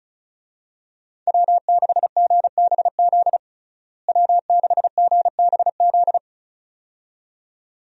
Morse code, heard as a keyed tone, about 36 wpm, 700 Hz. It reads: W6GBZ W6GBZ
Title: USA Call Signs - 5 Characters - Repeated 2x - 35wpm